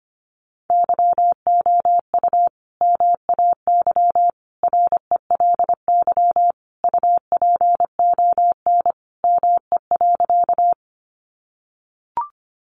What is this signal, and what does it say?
Morse code, heard as a keyed tone, about 25 wpm, 700 Hz. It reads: YOU MAY RELY UPON ME.
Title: Sentences from Top 1200 Words 25wpm